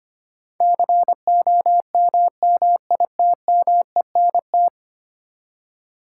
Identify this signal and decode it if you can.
Morse code, heard as a keyed tone, about 25 wpm, 700 Hz. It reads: COMMITMENT
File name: Ten Letter Words - 25wpm